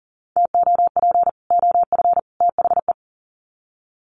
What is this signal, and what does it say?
Morse code, heard as a keyed tone, about 40 wpm, 700 Hz. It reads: TOP OF THE
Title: Sets of 3 Words 40wpm